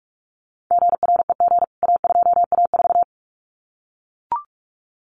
Morse code, heard as a keyed tone, about 44 wpm, 700 Hz. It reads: GREG A2A4
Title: CWT Exchange 45wpm